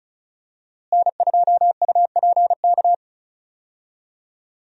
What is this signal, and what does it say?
Morse code, heard as a keyed tone, about 36 wpm, 700 Hz. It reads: N2UPK
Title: USA Call Signs - 5 Characters - Repeated 3x - 35wpm